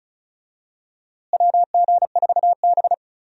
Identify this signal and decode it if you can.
Morse code, heard as a keyed tone, about 36 wpm, 700 Hz. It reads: WG4B
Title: USA Call Signs - 4 Characters - Repeated 2x - 35wpm